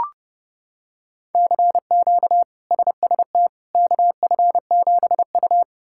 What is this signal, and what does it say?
Morse code, heard as a keyed tone, about 32 wpm, 700 Hz. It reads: CQ SST KF7U
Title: K1USN SST QSO Practice - 30wpm